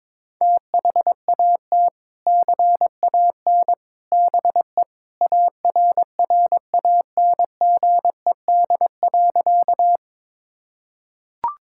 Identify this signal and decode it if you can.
Morse code, heard as a keyed tone, about 22 wpm, 700 Hz. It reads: THAT CAN BE ARRANGED.
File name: practice / 22wpm / Sentences with 3-Word Phrases - 22wpm